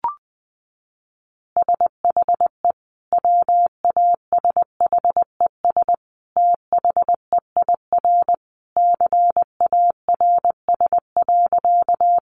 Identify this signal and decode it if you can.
Morse code, heard as a keyed tone, about 20 wpm, 700 Hz. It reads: SHE WASHES THEIR CARS.
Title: Easy Phrases 20wpm